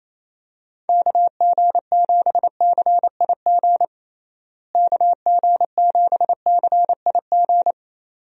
Morse code, heard as a keyed tone, about 28 wpm, 700 Hz. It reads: KG7CIG KG7CIG
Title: USA Call Signs - 6 Characters - Repeated 2x - 28wpm